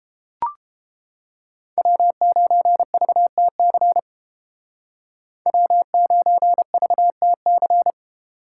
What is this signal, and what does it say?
Morse code, heard as a keyed tone, about 29 wpm, 700 Hz. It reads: W9VTC W9VTC